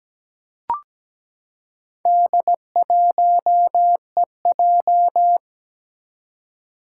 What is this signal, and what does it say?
Morse code, heard as a keyed tone, about 17 wpm, 700 Hz. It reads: D1EJ
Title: Four Challenging Letter-Number Combinations 17wpm